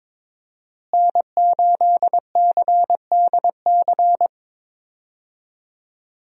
Morse code, heard as a keyed tone, about 22 wpm, 700 Hz. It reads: N8CDC
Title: USA Call Signs - 5 Characters - Repeated 2x - 22wpm